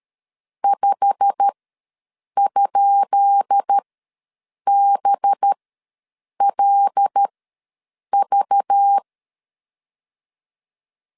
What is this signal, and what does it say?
Morse code, heard as a keyed tone, about 13 wpm, spaced as for 7 wpm, 800 Hz. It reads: 5?BLV
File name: LEZIONI / 10 SIGNS / 10B INT ws 12wpm 3527